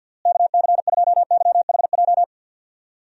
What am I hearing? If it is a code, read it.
Morse code, heard as a keyed tone, about 51 wpm, 700 Hz. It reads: KC2YHJ
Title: USA Call Signs - 6 Characters - Repeated 3x - 50wpm